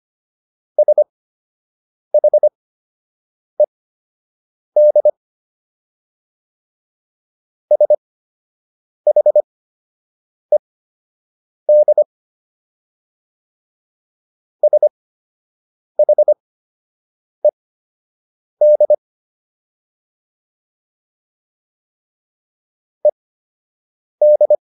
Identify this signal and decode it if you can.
Morse code, heard as a keyed tone, about 25 wpm, spaced as for 7 wpm, 600 Hz. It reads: SHED SHED SHED ED